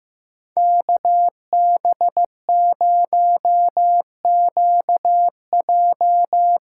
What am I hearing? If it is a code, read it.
Morse code, heard as a keyed tone, about 15 wpm, 700 Hz. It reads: KB0QJ
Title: USA Call Signs - 6 Characters - 8x Character Spacing - 15wpm